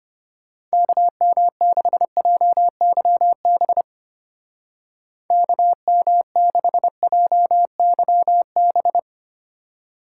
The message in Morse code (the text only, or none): KM6JYB KM6JYB